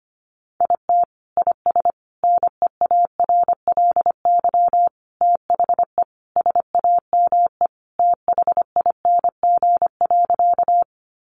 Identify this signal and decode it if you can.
Morse code, heard as a keyed tone, about 25 wpm, 700 Hz. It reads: IT IS NEARLY THE SAME THING.
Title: Sentences from Top 1300 Words 25wpm